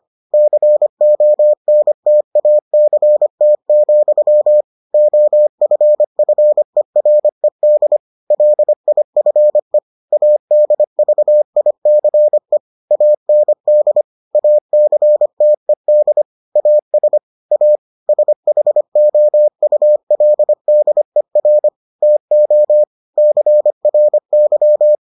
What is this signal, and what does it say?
Morse code, heard as a keyed tone, about 25 wpm, 600 Hz. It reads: CONTACT, OFFERED LIFE ADVICE AND ACTED AS A SHOULDER TO CRY